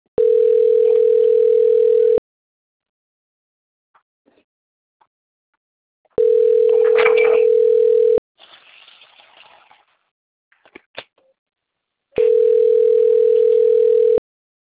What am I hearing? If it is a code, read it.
Ringback tone (North America), 440 + 480 Hz.